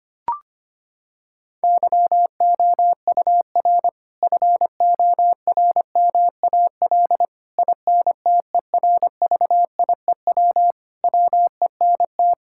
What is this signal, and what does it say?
Morse code, heard as a keyed tone, about 25 wpm, 700 Hz. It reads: YOUR FORMAL INTERVIEW WENT
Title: Sentences from Top 1600 Words 25wpm